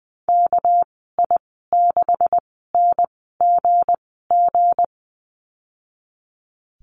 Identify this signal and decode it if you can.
Morse code, heard as a keyed tone, about 20 wpm, spaced as for 14 wpm, 700 Hz. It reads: KI6NGG
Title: 213 Lesson 34g - Number 8 - Call Signs 20wpm